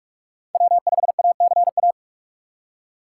Morse code, heard as a keyed tone, about 46 wpm, 700 Hz. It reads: W5ACA
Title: USA Call Signs - 5 Characters - 4x Character Spacing - 45wpm